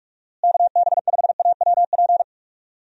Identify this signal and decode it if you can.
Morse code, heard as a keyed tone, about 46 wpm, 700 Hz. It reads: KB5AWP
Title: USA Call Signs - 6 Characters - Repeated 2x - 45wpm